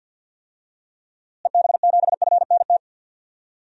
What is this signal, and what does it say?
Morse code, heard as a keyed tone, about 48 wpm, 700 Hz. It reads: EB7FNT
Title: DXCC-100 Call Signs - 6 Characters - Repeated 2x - 50wpm